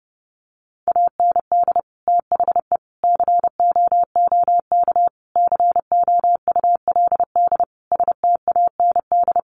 Morse code, heard as a keyed tone, about 30 wpm, 700 Hz. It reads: AND THE COOK COULD STAND
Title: Sentences from Top 1300 Words 30wpm